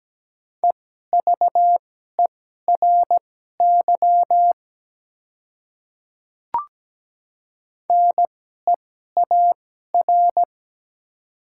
Morse code, heard as a keyed tone, about 17 wpm, spaced as for 12 wpm, 700 Hz. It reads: EVERY NEAR
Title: 2x Character Spacing - Top 200 Words - 17wpm